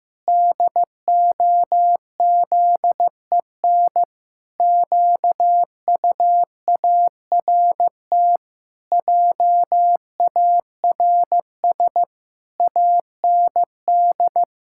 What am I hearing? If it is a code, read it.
Morse code, heard as a keyed tone, about 15 wpm, 700 Hz. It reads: DOZEN QUART JARS AND
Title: Pangram Challenge v2 - 15wpm